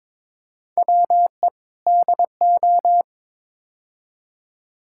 Morse code, heard as a keyed tone, about 22 wpm, 700 Hz. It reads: WE DO
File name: practice / 22wpm / Build A Phrase - 5-Word Phrases - 22wpm